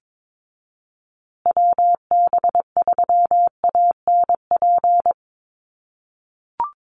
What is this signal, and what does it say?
Morse code, heard as a keyed tone, about 22 wpm, 700 Hz. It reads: WB3ANP
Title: USA Call Signs Repeated 2x 22wpm